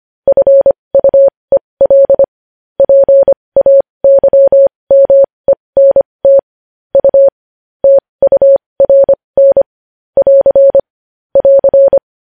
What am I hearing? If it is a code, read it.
Morse code, heard as a keyed tone, about 25 wpm, spaced as for 20 wpm, 550 Hz. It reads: FUEL PAYMENT U TURN <AR> <AR>
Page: News Headlines in Morse Code at 20 WPM